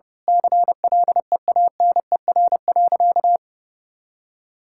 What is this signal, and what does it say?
Morse code, heard as a keyed tone, about 29 wpm, 700 Hz. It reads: CLEANER.
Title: Pangram Challenge v2 - 30wpm